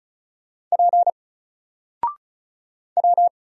Morse code, heard as a keyed tone, about 34 wpm, 700 Hz. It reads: P W
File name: Single-Letter-Number-35wpm